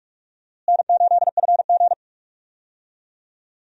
Morse code, heard as a keyed tone, about 45 wpm, 700 Hz. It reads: N8FG
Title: USA Call Signs - 4 Characters - Repeated 2x - 45wpm